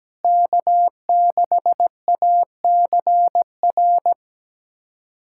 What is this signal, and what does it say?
Morse code, heard as a keyed tone, about 17 wpm, 700 Hz. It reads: K6ACR
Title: USA Callsign DE Callsign - 5 Characters - 17wpm